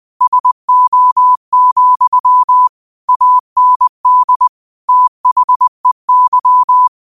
Morse code, heard as a keyed tone, about 20 wpm, 1000 Hz. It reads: SO, AND THEY
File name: qotdfiles / 20wpm